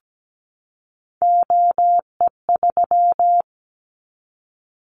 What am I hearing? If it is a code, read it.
Morse code, heard as a keyed tone, about 17 wpm, 700 Hz. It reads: OE3
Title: Callsign Prefixes for Top 25 DXCC - 17wpm